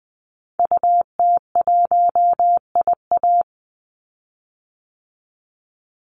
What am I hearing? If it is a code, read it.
Morse code, heard as a keyed tone, about 20 wpm, 700 Hz. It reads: UT1IA
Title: Call Signs Repeated 3x - 20wpm